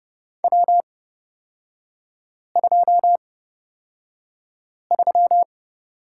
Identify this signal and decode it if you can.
Morse code, heard as a keyed tone, about 30 wpm, spaced as for 5 wpm, 700 Hz. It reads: W23
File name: Lesson 37e - Letter x - Three Characters 30wpm